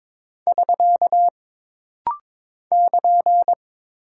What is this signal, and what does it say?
Morse code, heard as a keyed tone, about 22 wpm, 700 Hz. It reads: <SK> <KN>
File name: practice / 22wpm / Single Punctuation-Prosign - Rapid-Fire - 22wpm